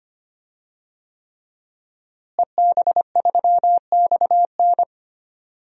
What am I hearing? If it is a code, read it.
Morse code, heard as a keyed tone, about 25 wpm, 700 Hz. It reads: EB3XN
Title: DXCC-100 Call Signs - 5 Characters - Repeated 3x - 25wpm